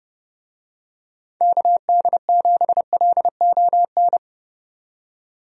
Morse code, heard as a keyed tone, about 30 wpm, 700 Hz. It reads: KD7LON
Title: USA Call Signs - 6 Characters - 30wpm